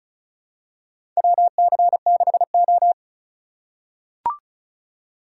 Morse code, heard as a keyed tone, about 34 wpm, 700 Hz. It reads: WC6O
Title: USA Call Signs - 4 Characters - 4x Character Spacing - 35wpm